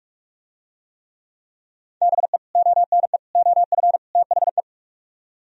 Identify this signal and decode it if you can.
Morse code, heard as a keyed tone, about 44 wpm, 700 Hz. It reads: BE ONE OF THE